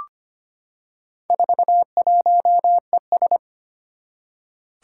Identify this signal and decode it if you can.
Morse code, heard as a keyed tone, about 25 wpm, 700 Hz. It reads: 41ES
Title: Four Challenging Letter-Number Combinations 25wpm